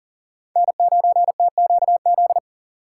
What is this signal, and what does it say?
Morse code, heard as a keyed tone, about 40 wpm, 700 Hz. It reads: N9TQZ
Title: USA Call Signs - 5 Characters - 2x Character Spacing - 40wpm